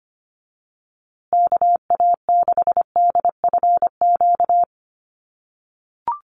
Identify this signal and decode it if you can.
Morse code, heard as a keyed tone, about 25 wpm, 700 Hz. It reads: KA6DFQ
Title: USA Call Signs Repeated 2x 25wpm